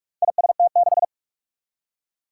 Morse code, heard as a keyed tone, about 44 wpm, 700 Hz. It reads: IST6
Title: Four Challenging Letter-Number Combinations 45wpm